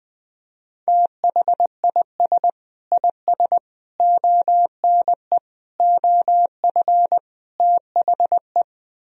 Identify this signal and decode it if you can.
Morse code, heard as a keyed tone, about 20 wpm, 700 Hz. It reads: THIS IS ONE OF THE
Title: Speed-Racing - Sets of 5 Words 17wpm